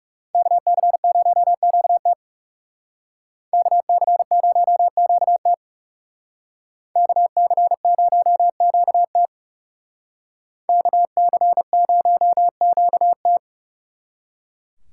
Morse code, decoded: KC0QT KC0QT KC0QT KC0QT

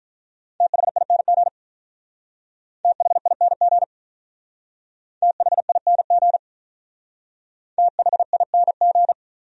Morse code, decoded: THING THING THING THING

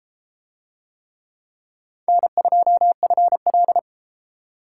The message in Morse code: N2FL

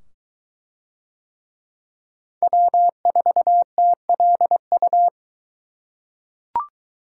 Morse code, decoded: W4TLU

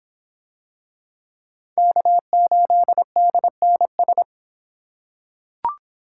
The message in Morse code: K8DNS

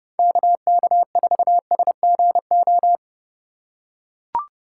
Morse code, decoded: KK4SGO

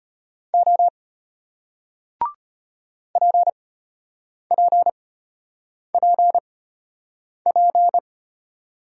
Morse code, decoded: O PPPP

